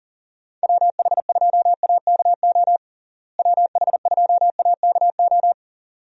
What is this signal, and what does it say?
Morse code, heard as a keyed tone, about 40 wpm, 700 Hz. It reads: WH2AKO WH2AKO
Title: USA Call Signs - 6 Characters - Repeated 2x - 40wpm